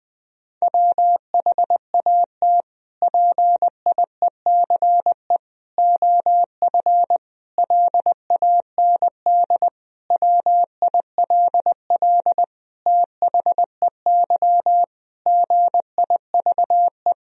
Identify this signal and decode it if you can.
Morse code, heard as a keyed tone, about 20 wpm, 700 Hz. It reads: WHAT PIECE OF LAND WILL THEY GIVE
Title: Sentences from Top 1000 Words - Review Part 2 - 20wpm